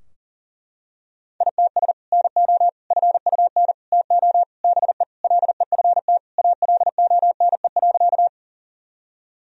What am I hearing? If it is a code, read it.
Morse code, heard as a keyed tone, about 40 wpm, 700 Hz. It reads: ITS NO FUN TO BE LEFT ALONE.
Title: Sentences from Top 1000 Words - Review Part 4 - 40wpm